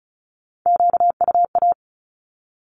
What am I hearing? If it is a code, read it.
Morse code, heard as a keyed tone, about 36 wpm, 700 Hz. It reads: QUA
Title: Q-Codes with Spoken Meaning - 35wpm